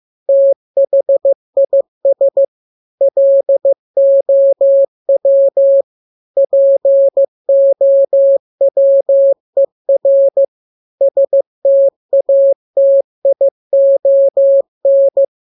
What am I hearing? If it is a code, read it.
Morse code, heard as a keyed tone, about 15 wpm, 550 Hz. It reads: THIS LOW POWER STATION
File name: jokes / joke 4 15 0000